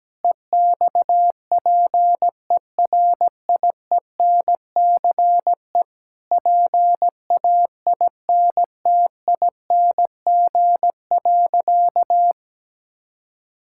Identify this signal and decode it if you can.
Morse code, heard as a keyed tone, about 17 wpm, 700 Hz. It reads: EXPERIENCE PAINTING.
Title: Sentences from Top 1000 Words - Review Part 6 - 17wpm